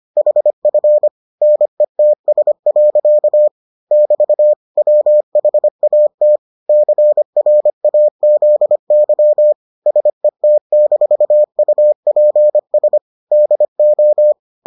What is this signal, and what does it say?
Morse code, heard as a keyed tone, about 25 wpm, 600 Hz. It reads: HF NETS. <BT> WHAT CRAZY SET-UPS DO